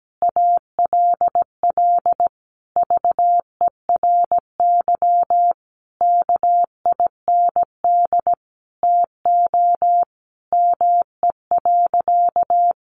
Morse code, heard as a keyed tone, about 17 wpm, 700 Hz. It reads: ALL VERY KIND TO ME.
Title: Sentences from Top 200 Words 17wpm